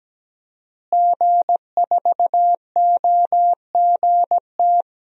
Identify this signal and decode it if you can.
Morse code, heard as a keyed tone, about 17 wpm, 700 Hz. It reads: G4OGT
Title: DXCC-100 Call Signs - 5 Characters - Repeated 2x - 17wpm